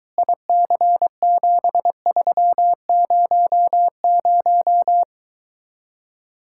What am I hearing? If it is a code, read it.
Morse code, heard as a keyed tone, about 23 wpm, 700 Hz. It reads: IC7300